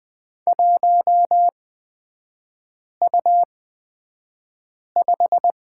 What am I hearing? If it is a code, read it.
Morse code, heard as a keyed tone, about 20 wpm, spaced as for 5 wpm, 700 Hz. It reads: 1U5
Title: 236 Lesson 38e - Letter q - Three Characters 20wpm